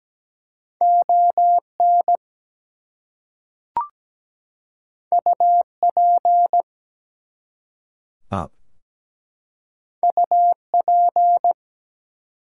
Morse code, heard as a keyed tone, about 17 wpm, 700 Hz. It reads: ON UP UP
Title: Two Letter Words - 17wpm